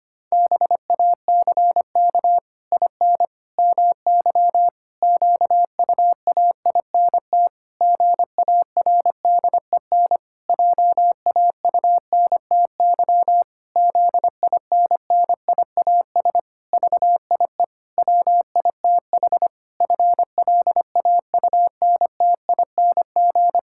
Morse code, heard as a keyed tone, about 25 wpm, 700 Hz. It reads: BACK IN MY QUAINT GARDEN JAUNTY ZINNIAS VIE WITH FLAUNTING
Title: Pangram Challenge v2 - 25wpm